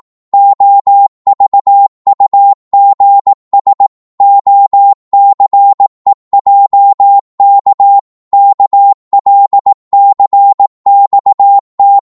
Morse code, heard as a keyed tone, about 18 wpm, 800 Hz. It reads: OVUGS OCEJK KLCXT
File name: Morse / ltr-18wpm